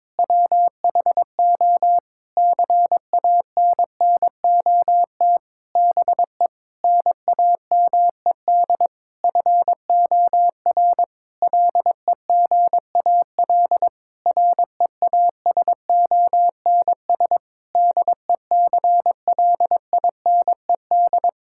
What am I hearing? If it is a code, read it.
Morse code, heard as a keyed tone, about 22 wpm, 700 Hz. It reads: WHO CANNOT BE NAMED FOR LEGAL REASONS DECLINED
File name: practice / 22wpm / Sentences with 8-Word Phrases - 22wpm